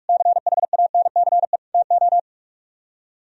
Morse code, heard as a keyed tone, about 55 wpm, 700 Hz. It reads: CHANCE TO
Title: Speed-Racing - Sets of 4 Words 45wpm